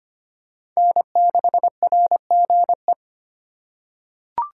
Morse code, heard as a keyed tone, about 25 wpm, 700 Hz. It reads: N6RGE
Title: USA Call Signs - 5 Characters - 3x Character Spacing - 25wpm